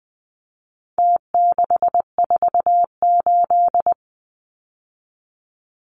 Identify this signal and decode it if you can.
Morse code, heard as a keyed tone, about 20 wpm, 700 Hz. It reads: T648
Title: Numbers - Contest Style - 4-Digit 20wpm